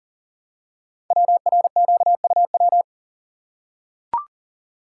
Morse code, heard as a keyed tone, about 39 wpm, 700 Hz. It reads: WRQUW
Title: Code Groups - 5 Letters - 40wpm